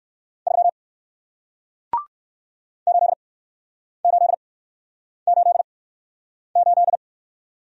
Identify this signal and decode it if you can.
Morse code, heard as a keyed tone, about 60 wpm, spaced as for 9 wpm, 700 Hz. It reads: 3 8888